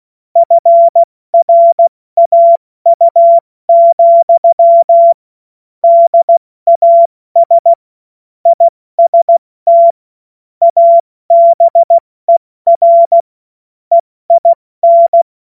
Morse code, heard as a keyed tone, about 16 wpm, 680 Hz. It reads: FRAU, DAS IST ABER EIN
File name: Morsekurs / Witze / Witz 05 Dezember 2023 70 F80